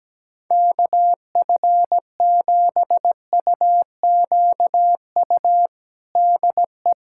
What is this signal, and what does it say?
Morse code, heard as a keyed tone, about 17 wpm, 700 Hz. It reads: KF7UQU DE
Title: USA Callsign DE Callsign - 6 Characters - 17wpm